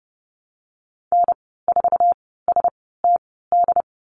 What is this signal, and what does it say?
Morse code, heard as a keyed tone, about 30 wpm, spaced as for 17 wpm, 700 Hz. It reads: N4STD
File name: Lesson 14g - Number 2 - Call Signs 30wpm